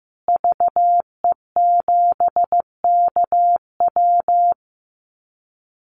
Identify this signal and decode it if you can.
Morse code, heard as a keyed tone, about 15 wpm, 700 Hz. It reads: VE7KW
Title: Call Signs Repeated 2x - 15wpm